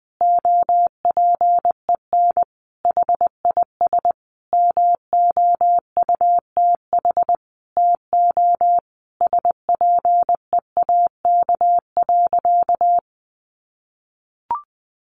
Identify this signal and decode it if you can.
Morse code, heard as a keyed tone, about 20 wpm, 700 Hz. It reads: OPEN HIS MOUTH TO SPEAK.
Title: Sentences from Top 1500 Words 20wpm